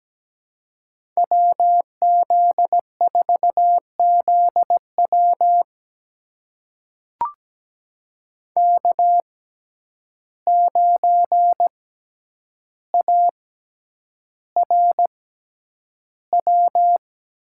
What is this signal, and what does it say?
Morse code, heard as a keyed tone, about 17 wpm, spaced as for 9 wpm, 700 Hz. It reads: WZ4ZW K9ARW
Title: USA Call Signs - 5 Characters - 6x Character Spacing - 17wpm